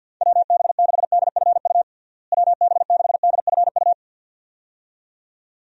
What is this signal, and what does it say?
Morse code, heard as a keyed tone, about 48 wpm, 700 Hz. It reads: WB6DFU WB6DFU
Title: USA Call Signs - 6 Characters - Repeated 2x - 50wpm